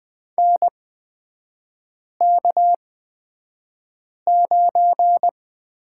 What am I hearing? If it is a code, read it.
Morse code, heard as a keyed tone, about 20 wpm, spaced as for 5 wpm, 700 Hz. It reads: NK9
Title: 250 Lesson 40e - BK - Three Characters 20wpm